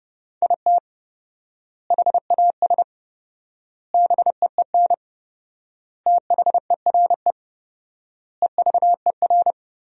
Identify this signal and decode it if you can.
Morse code, heard as a keyed tone, about 32 wpm, 700 Hz. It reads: IT HAS BEEN THERE EVER
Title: 4x Word-Spacing - Sentences from Top 400 Words - 30wpm